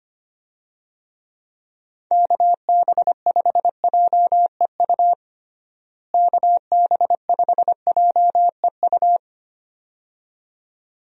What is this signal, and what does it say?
Morse code, heard as a keyed tone, about 25 wpm, 700 Hz. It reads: KB5JEU KB5JEU
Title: USA Call Signs - 6 Characters - Repeated 2x - 25wpm